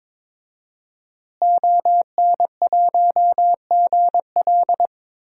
Morse code, heard as a keyed tone, about 22 wpm, 700 Hz. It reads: ON1GL
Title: DXCC-100 Call Signs - 5 Characters - Repeated 2x - 22wpm